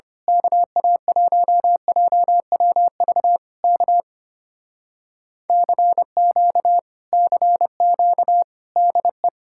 Morse code, heard as a keyed tone, about 24 wpm, 700 Hz. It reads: KA1JWV K CQ CQ DE
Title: Speed-Racing Common QSO Phrases - 25wpm